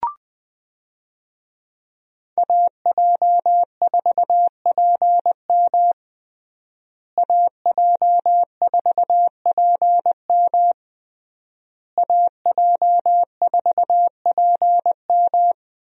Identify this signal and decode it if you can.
Morse code, heard as a keyed tone, about 20 wpm, 700 Hz. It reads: AJ4PM AJ4PM AJ4PM
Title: USA Call Signs - 5 Characters - Repeated 3x - 20wpm